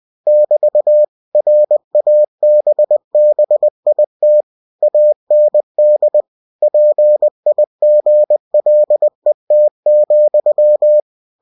Morse code, heard as a keyed tone, about 20 wpm, 600 Hz. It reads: <BT> RABBIT AND PIGLET,
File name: cw-books / winnie-the-pooh / winnie-20wpm-page-0066